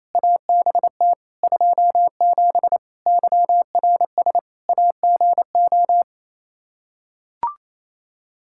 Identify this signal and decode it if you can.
Morse code, heard as a keyed tone, about 28 wpm, 700 Hz. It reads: ABT 27 YRS AGO
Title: Practice QSO - Exchange 3 - More Info - 28wpm